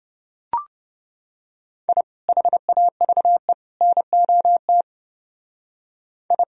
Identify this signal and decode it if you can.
Morse code, heard as a keyed tone, about 29 wpm, 700 Hz. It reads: I HAVE NOT I